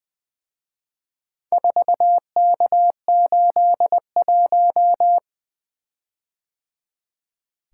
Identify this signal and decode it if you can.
Morse code, heard as a keyed tone, about 23 wpm, 700 Hz. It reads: 4K81
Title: Speed-Racing - Four Letter-Number Combinations - 20wpm